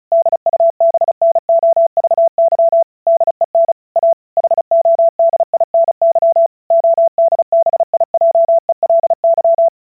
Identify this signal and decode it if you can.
Morse code, heard as a keyed tone, about 36 wpm, 670 Hz. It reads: DUBNOVY DEN A HODINY ODBIJELY
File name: vyuka cw / 35wpm / ukazka 35wpm